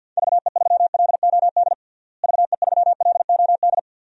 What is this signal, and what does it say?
Morse code, heard as a keyed tone, about 52 wpm, 700 Hz. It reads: VE3LOD VE3LOD
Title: DXCC-100 Call Signs - 6 Characters - Repeated 2x - 50wpm